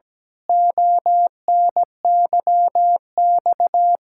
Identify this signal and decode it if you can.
Morse code, heard as a keyed tone, about 17 wpm, 700 Hz. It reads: ONYX
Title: Pangram Challenge v2 - 17wpm